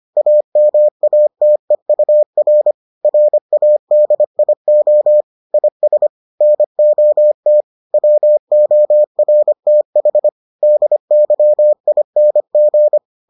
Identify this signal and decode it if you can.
Morse code, heard as a keyed tone, about 25 wpm, 600 Hz. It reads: AMATEUR RADIO IS NOT WORTH DYING